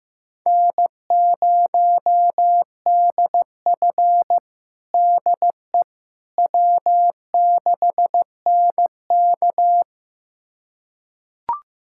Morse code, heard as a keyed tone, about 16 wpm, 700 Hz. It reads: N0DF DE W6NK
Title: USA Callsign DE Callsign - 4 Characters - 15wpm